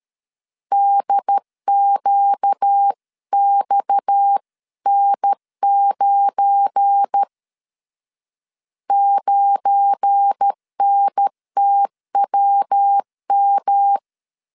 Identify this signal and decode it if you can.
Morse code, heard as a keyed tone, about 13 wpm, 800 Hz. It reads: DQXN9 9NTWM
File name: LEZIONI / 7 GF9Z / 7 GF9 12wpm 2319